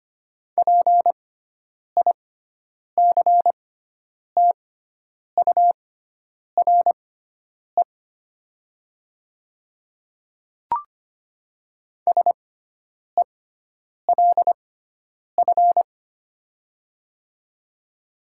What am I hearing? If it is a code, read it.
Morse code, heard as a keyed tone, about 25 wpm, spaced as for 9 wpm, 700 Hz. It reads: PICTURE SELF
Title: 6x Character Spacing - Top 200 Words - 25wpm